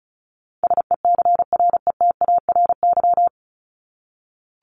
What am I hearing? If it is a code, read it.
Morse code, heard as a keyed tone, about 34 wpm, 700 Hz. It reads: SECRETARY
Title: Top 1251-1500 Words 35wpm